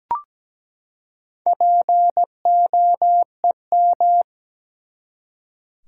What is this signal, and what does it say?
Morse code, heard as a keyed tone, about 17 wpm, 700 Hz. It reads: POEM
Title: Four Letter Words - 17wpm